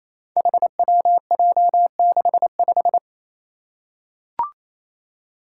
Morse code, heard as a keyed tone, about 28 wpm, 700 Hz. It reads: HWJ65
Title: Five Challenging Letter-Number Combinations 28wpm